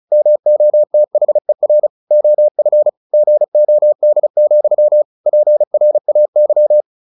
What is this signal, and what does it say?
Morse code, heard as a keyed tone, about 34 wpm, 600 Hz. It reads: MOTHER OF GOD, PRAY
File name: the old man and the sea / the old man and the sea 1 35wpm